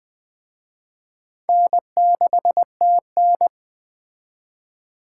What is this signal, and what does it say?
Morse code, heard as a keyed tone, about 22 wpm, 700 Hz. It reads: N6TN